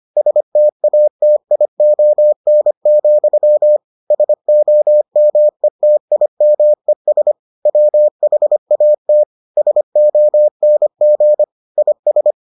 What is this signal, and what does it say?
Morse code, heard as a keyed tone, about 25 wpm, 600 Hz. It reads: STATION, SOMETIMES WHAT SONG IS